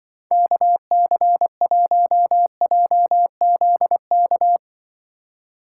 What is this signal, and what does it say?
Morse code, heard as a keyed tone, about 25 wpm, 700 Hz. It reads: KC1JZK
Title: Speed Racing - USA Call Signs 20wpm